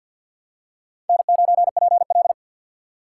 Morse code, heard as a keyed tone, about 52 wpm, 700 Hz. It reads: N9PL
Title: USA Call Signs - 4 Characters - 2x Character Spacing - 50wpm